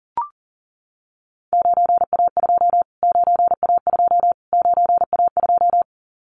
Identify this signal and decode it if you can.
Morse code, heard as a keyed tone, about 40 wpm, 700 Hz. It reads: 9A2 9A2 9A2
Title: Callsign Prefixes for Top 25 DXCC - Repeated 3x - 40wpm